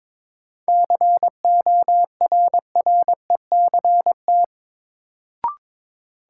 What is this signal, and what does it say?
Morse code, heard as a keyed tone, about 22 wpm, 700 Hz. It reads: CORRECT
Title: Sentences from Top 700 Words - New Word Drill - 22wpm